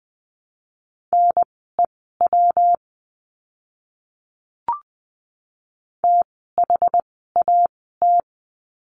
Morse code, heard as a keyed tone, about 20 wpm, spaced as for 14 wpm, 700 Hz. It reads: NEW THAT
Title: 206 Lesson 33f - Letter j - Words 20wpm